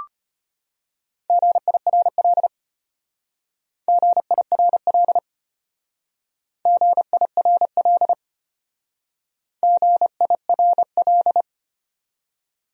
Morse code, heard as a keyed tone, about 32 wpm, 700 Hz. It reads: GIRL GIRL GIRL GIRL